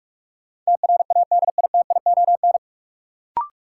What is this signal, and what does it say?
Morse code, heard as a keyed tone, about 42 wpm, 700 Hz. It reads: TRADITION
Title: Sentences from Top 1500 Words - New Word Drill Alt - 45wpm